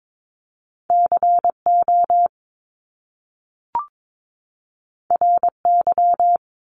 Morse code, heard as a keyed tone, about 22 wpm, 700 Hz. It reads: CO RY
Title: Top Two Letter Combinations in Words 22wpm